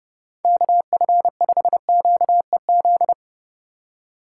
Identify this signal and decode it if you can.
Morse code, heard as a keyed tone, about 29 wpm, 700 Hz. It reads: KF5QEZ